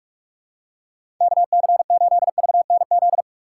Morse code, heard as a keyed tone, about 47 wpm, 700 Hz. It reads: KC8VNZ
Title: USA Call Signs - 6 Characters - 4x Character Spacing - 45wpm